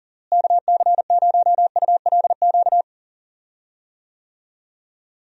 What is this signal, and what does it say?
Morse code, heard as a keyed tone, about 40 wpm, 700 Hz. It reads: KC0ULQ
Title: USA Call Signs - 6 Characters - Repeated 3x - 40wpm